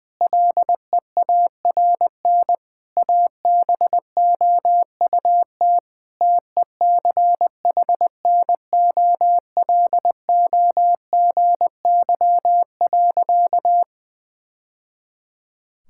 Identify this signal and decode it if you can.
Morse code, heard as a keyed tone, about 20 wpm, 700 Hz. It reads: LEARN ABOUT TECHNOLOGY.
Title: Sentences from Top 1000 Words - Review Part 5 - 20wpm